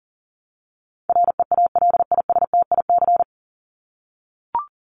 Morse code, heard as a keyed tone, about 40 wpm, 700 Hz. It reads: REALISTIC
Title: Top 1751-2000 Words 40wpm